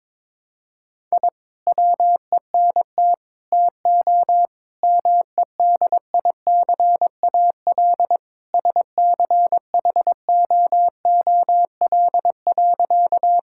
Morse code, heard as a keyed tone, about 21 wpm, 700 Hz. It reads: I WENT TO MEDICAL SCHOOL.
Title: Sentences from Top 1000 Words - Review Part 5 - 22wpm